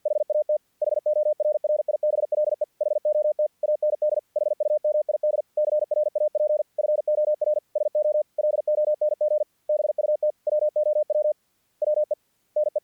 Morse code, heard as a keyed tone, about 48 wpm, 600 Hz. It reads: HAT HORRIBLE HOT AND HUMID CRAP FOR SO LONG BUT WOW WE NE
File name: archived / CRSnet